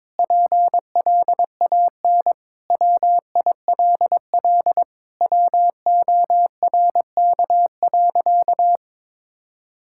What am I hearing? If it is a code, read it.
Morse code, heard as a keyed tone, about 22 wpm, 700 Hz. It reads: PLAN WILL WORK.
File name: practice / 22wpm / Sentences with 3-Word Phrases - 22wpm